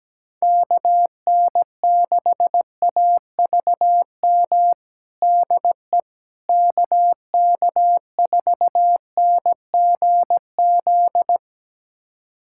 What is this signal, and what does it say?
Morse code, heard as a keyed tone, about 17 wpm, 700 Hz. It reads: KN6AVM DE KK4NGZ
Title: USA Callsign DE Callsign 17wpm